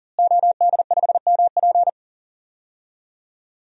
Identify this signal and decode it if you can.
Morse code, heard as a keyed tone, about 40 wpm, 700 Hz. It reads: OD5MP
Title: DXCC-100 Call Signs - 5 Characters - Repeated 2x - 40wpm